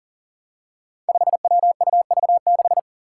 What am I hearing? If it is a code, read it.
Morse code, heard as a keyed tone, about 37 wpm, 700 Hz. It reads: 5WUV6
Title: Five Challenging Letter-Number Combinations 40wpm